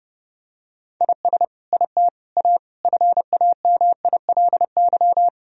Morse code, heard as a keyed tone, about 30 wpm, 700 Hz. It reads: IS IT A FAMILY
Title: Sentences from Top 1000 Words - Review Part 4 - 30wpm